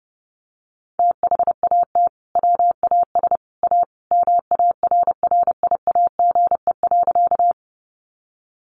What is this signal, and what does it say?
Morse code, heard as a keyed tone, about 30 wpm, 700 Hz. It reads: THAT WAS A MARRIAGE.
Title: Sentences from Top 800 Words 30wpm